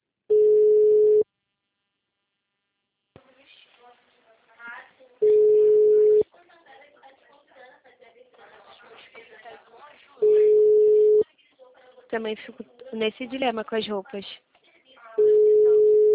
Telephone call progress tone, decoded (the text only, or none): ringback tone